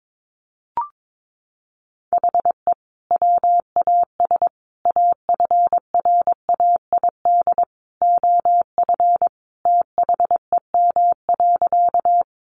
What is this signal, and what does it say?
Morse code, heard as a keyed tone, about 22 wpm, 700 Hz. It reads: HE WAS AFRAID OF THEM.
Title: Sentences from Top 1500 Words 22wpm